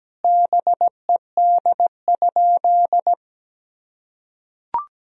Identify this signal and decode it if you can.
Morse code, heard as a keyed tone, about 17 wpm, 700 Hz. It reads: BED?
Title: Sentences from Top 1000 Words - Review Part 1 - 17wpm